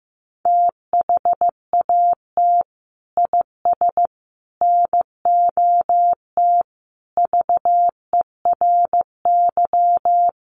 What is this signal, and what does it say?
Morse code, heard as a keyed tone, about 15 wpm, 700 Hz. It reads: THAT IS NOT VERY
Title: Sentences from Top 600 Words 15wpm